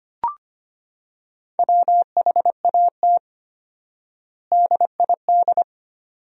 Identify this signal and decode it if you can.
Morse code, heard as a keyed tone, about 25 wpm, 700 Hz. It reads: WHAT DID
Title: 4x Word-Spacing - Sentences from Top 100 Words - 25wpm